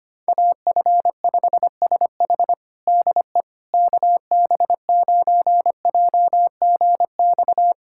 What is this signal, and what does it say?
Morse code, heard as a keyed tone, about 25 wpm, 700 Hz. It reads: AF5SH DE KB9JGX
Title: USA Callsign DE Callsign 25wpm